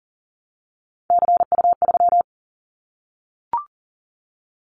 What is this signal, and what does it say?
Morse code, heard as a keyed tone, about 38 wpm, 700 Hz. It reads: CU3
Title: Callsign Prefixes for Top 50 DXCC - 40wpm